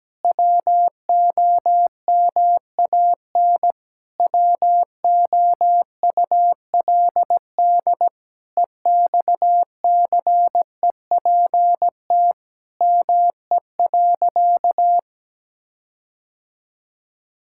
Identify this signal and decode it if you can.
Morse code, heard as a keyed tone, about 17 wpm, 700 Hz. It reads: WOMAN WOULD EXCEPT ME.
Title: Sentences from Top 1600 Words 17wpm